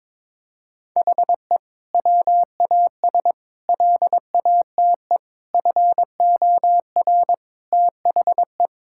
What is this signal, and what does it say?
Morse code, heard as a keyed tone, about 22 wpm, 700 Hz. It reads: HE WAS LATE FOR THE
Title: Sentences from Top 1600 Words 22wpm